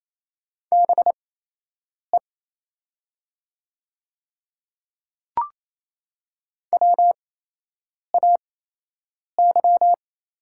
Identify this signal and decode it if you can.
Morse code, heard as a keyed tone, about 28 wpm, spaced as for 8 wpm, 700 Hz. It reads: BE WAY
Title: 8x Character Spacing - Top 200 Words - 28wpm